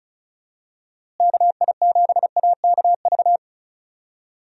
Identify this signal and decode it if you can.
Morse code, heard as a keyed tone, about 34 wpm, 700 Hz. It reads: KI7AKV
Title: USA Call Signs - 6 Characters - Repeated 2x - 35wpm